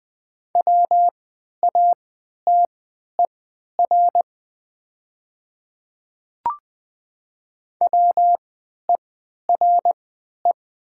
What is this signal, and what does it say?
Morse code, heard as a keyed tone, about 20 wpm, spaced as for 11 wpm, 700 Hz. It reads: WATER WERE
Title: 3x Character Spacing - Top 100 Words - 20wpm